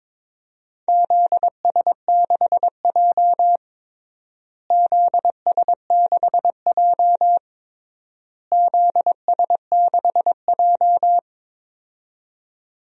Morse code, heard as a keyed tone, about 22 wpm, 700 Hz. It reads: ZS6J ZS6J ZS6J
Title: DXCC-100 Call Signs - 4 Characters - Repeated 3x - 22wpm